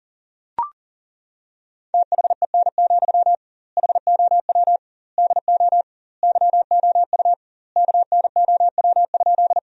Morse code, heard as a keyed tone, about 40 wpm, 700 Hz. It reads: THEN, HOW DO YOU KNOW?
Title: 2x Word-Spacing - Sentences from Top 100 Words - 40wpm